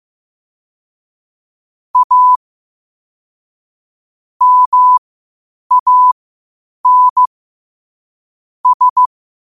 Morse code, heard as a keyed tone, about 15 wpm, spaced as for 8 wpm, 1000 Hz. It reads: A MAN S
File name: qotdfiles / 8wpm